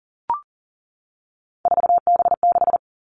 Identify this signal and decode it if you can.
Morse code, heard as a keyed tone, about 40 wpm, 700 Hz. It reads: 4B6
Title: Three Challenging Letter-Number Combinations 40wpm